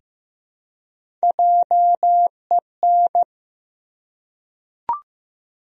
Morse code, heard as a keyed tone, about 15 wpm, 700 Hz. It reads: JEN
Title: USA Common Names - Max 3 Letters - 15wpm